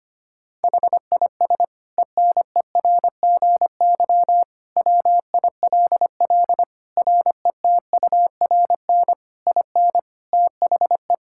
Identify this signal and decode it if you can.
Morse code, heard as a keyed tone, about 25 wpm, 700 Hz. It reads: HIS ENERGY WILL RETURN IN THE
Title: Sentences from Top 1000 Words - Review Part 6 - 25wpm